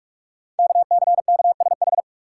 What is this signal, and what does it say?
Morse code, heard as a keyed tone, about 44 wpm, 690 Hz. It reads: KCKSH
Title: Code Groups - 5 Letters - 45wpm